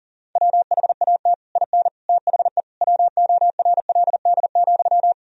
Morse code, heard as a keyed tone, about 40 wpm, 700 Hz. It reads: WHAT IN THE WORLD,
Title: Sentences from Top 1000 Words - Review Part 1 - 40wpm